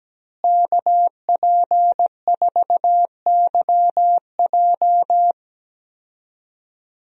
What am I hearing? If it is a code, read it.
Morse code, heard as a keyed tone, about 17 wpm, 700 Hz. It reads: KP4YJ
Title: DXCC-100 Call Signs - 5 Characters - Repeated 2x - 17wpm